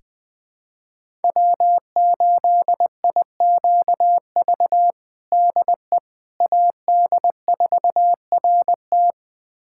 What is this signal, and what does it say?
Morse code, heard as a keyed tone, about 20 wpm, 700 Hz. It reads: W8IQV DE AD4RT
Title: USA Callsign DE Callsign 20wpm